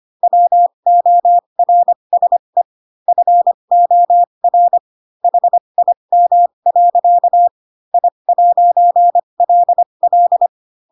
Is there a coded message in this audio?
Yes: Morse code from 0.2 to 10.5 s, heard as a keyed tone, about 25 wpm, spaced as for 22 wpm, 700 Hz. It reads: WORSE FOR HIM. I'LL